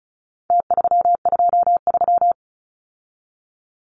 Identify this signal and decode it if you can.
Morse code, heard as a keyed tone, about 34 wpm, 700 Hz. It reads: T323
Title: Numbers - Contest Style - 4-Digit 35wpm